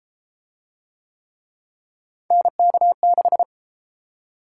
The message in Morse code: NK6

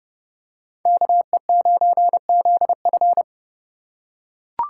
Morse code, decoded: KE9ZF

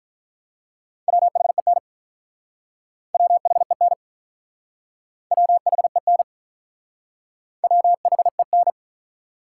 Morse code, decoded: WHEN WHEN WHEN WHEN

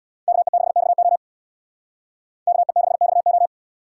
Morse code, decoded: KE6XO KE6XO

